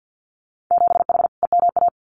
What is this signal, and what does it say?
Morse code, heard as a keyed tone, about 50 wpm, 700 Hz. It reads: 7H EMA
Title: Winter Field Day Exchanges Encore 50wpm